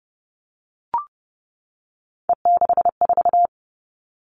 Morse code, heard as a keyed tone, about 30 wpm, 700 Hz. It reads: E64
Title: All Cut Numbers - 3-Digit 30wpm